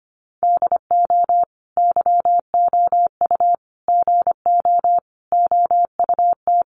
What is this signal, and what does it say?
Morse code, heard as a keyed tone, about 25 wpm, 700 Hz. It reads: DO YOU GO OUT
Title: Sentences from Top 700 Words 25wpm